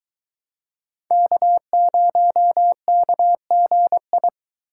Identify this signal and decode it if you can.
Morse code, heard as a keyed tone, about 24 wpm, 700 Hz. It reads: K0KGI